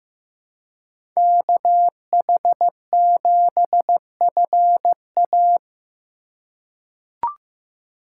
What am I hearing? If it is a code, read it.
Morse code, heard as a keyed tone, about 15 wpm, 700 Hz. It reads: KH7FA
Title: USA Call Signs - 5 Characters - 3x Character Spacing - 15wpm